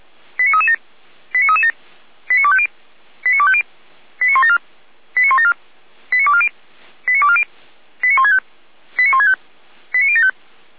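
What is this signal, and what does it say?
Five-tone selective call (ZVEI), heard as quick runs of five tones, tones of about 70 ms. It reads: 89337 89337 89250 89250 89174 89174 89240 89240 89165 89165 89075
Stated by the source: ZVEI five-tone selective call received with an SDR.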